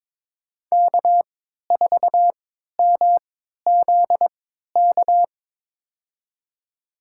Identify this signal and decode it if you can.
Morse code, heard as a keyed tone, about 22 wpm, spaced as for 13 wpm, 700 Hz. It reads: K4MZK
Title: USA Call Signs - 5 Characters - 3x Character Spacing - 22wpm